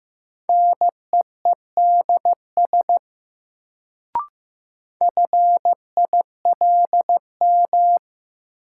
Morse code, heard as a keyed tone, about 15 wpm, 700 Hz. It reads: NEEDS FILM
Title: Sentences from Top 1000 Words - Review Part 3 - New Word Drill - 15wpm